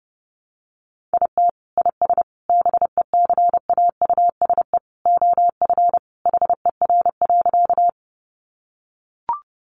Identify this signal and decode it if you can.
Morse code, heard as a keyed tone, about 30 wpm, 700 Hz. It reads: IT IS BECAUSE OF HER.
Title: Sentences from Top 600 Words 30wpm